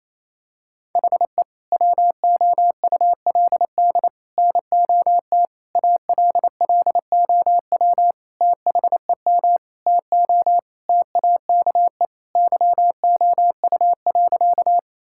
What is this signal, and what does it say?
Morse code, heard as a keyed tone, about 28 wpm, 700 Hz. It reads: HE WOULD NOT ALLOW THEM TO TAKE YOU.
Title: Sentences from Top 1000 Words - Review Part 5 - 28wpm